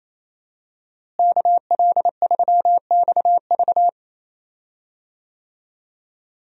Morse code, heard as a keyed tone, about 27 wpm, 700 Hz. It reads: KL3XV
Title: USA Call Signs - 5 Characters - Repeated 3x - 28wpm